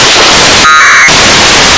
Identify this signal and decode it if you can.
Packet radio (AX.25, as in APRS).
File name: xw1 fm afsk